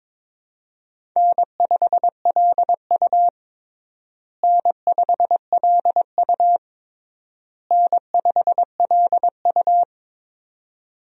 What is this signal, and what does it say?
Morse code, heard as a keyed tone, about 22 wpm, 700 Hz. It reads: N5LU N5LU N5LU
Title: USA Call Signs - 4 Characters - Repeated 3x - 22wpm